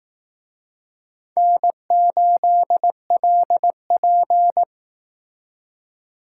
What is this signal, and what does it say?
Morse code, heard as a keyed tone, about 18 wpm, 700 Hz. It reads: N8LP
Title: Speed Racing - USA 1x2 Call Signs 15wpm